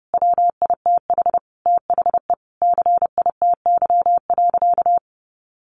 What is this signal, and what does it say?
Morse code, heard as a keyed tone, about 30 wpm, 700 Hz. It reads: WITH THE CITY.
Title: Sentences from Top 1500 Words 30wpm